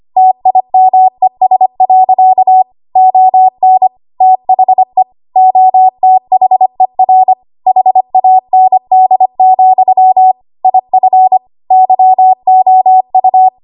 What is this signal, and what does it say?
Morse code, heard as a keyed tone, about 25 wpm, 750 Hz. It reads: TIMES. ON THE OTHER HAND, IF YOU
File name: Archive / 25 WPM / 25WPM